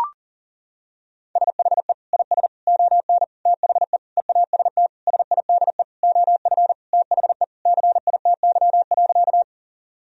Morse code, heard as a keyed tone, about 37 wpm, 700 Hz. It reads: SHE IS ON THE EAST SIDE OF THE CITY.
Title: Sentences from Top 1000 Words - Review Part 3 - 40wpm